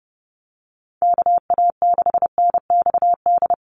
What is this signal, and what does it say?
Morse code, heard as a keyed tone, about 30 wpm, 700 Hz. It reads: KA6NXD
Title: USA Call Signs 30wpm